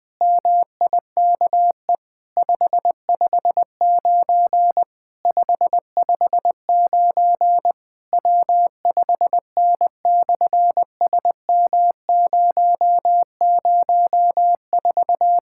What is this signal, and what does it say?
Morse code, heard as a keyed tone, about 20 wpm, 700 Hz. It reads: MIKE 559 559 W5N/SM004
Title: SOTA QSO Practice 20wpm